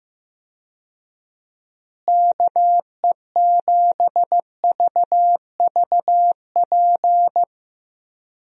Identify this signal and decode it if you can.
Morse code, heard as a keyed tone, about 15 wpm, 700 Hz. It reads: KE7VVP
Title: USA Call Signs - 6 Characters - Repeated 3x - 15wpm